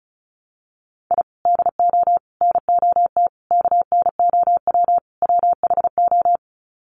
Morse code, heard as a keyed tone, about 34 wpm, 700 Hz. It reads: I DO NOT KNOW WHO
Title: Sets of 5 Words 35wpm